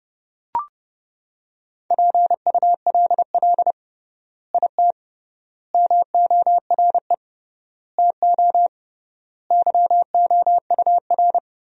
Morse code, heard as a keyed tone, about 30 wpm, 700 Hz. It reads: PULL IT MORE TO YOUR
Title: 3x Word-Spacing - Sentences from Top 400 Words - 30wpm